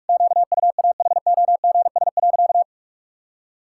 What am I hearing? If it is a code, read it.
Morse code, heard as a keyed tone, about 46 wpm, 700 Hz. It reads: QUAHOGS.
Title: Pangram Challenge v2 - 45wpm